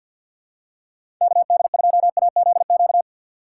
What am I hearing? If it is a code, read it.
Morse code, heard as a keyed tone, about 53 wpm, 700 Hz. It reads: KD2AZQ
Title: USA Call Signs - 6 Characters - 8x Character Spacing - 50wpm